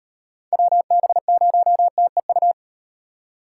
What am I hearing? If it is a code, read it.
Morse code, heard as a keyed tone, about 38 wpm, 700 Hz. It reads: WB0TEU